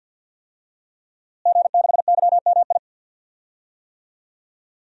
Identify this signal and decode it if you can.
Morse code, heard as a keyed tone, about 53 wpm, 700 Hz. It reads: G6YMI
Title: DXCC-100 Call Signs - 5 Characters - Repeated 2x - 50wpm